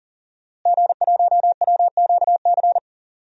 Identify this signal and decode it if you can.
Morse code, heard as a keyed tone, about 40 wpm, 700 Hz. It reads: G1WQC
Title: DXCC-100 Call Signs - 5 Characters - Repeated 2x - 40wpm